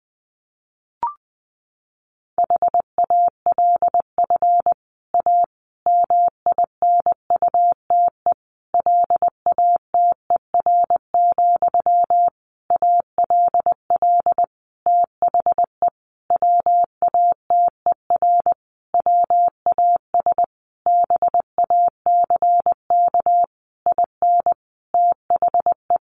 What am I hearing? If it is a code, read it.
Morse code, heard as a keyed tone, about 20 wpm, 700 Hz. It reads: HALF A MINUTE LATER, ALL THE WATER WAS BACK IN THE
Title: Sentences from Top 1100 Words 20wpm